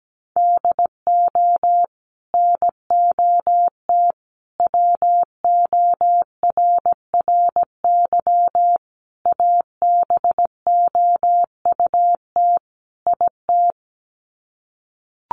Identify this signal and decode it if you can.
Morse code, heard as a keyed tone, about 17 wpm, 700 Hz. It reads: DO NOT WORRY ABOUT IT
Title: Sets of 5 Words 17wpm